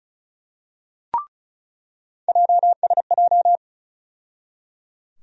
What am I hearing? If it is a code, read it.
Morse code, heard as a keyed tone, about 36 wpm, 700 Hz. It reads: JSJ
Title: USA Call Sign Postfixes - 3 Letters 35wpm